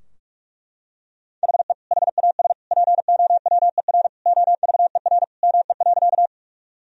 Morse code, heard as a keyed tone, about 42 wpm, 700 Hz. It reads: HE HAS POWER OVER ME.
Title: Sentences from Top 1000 Words - Review Part 2 - 45wpm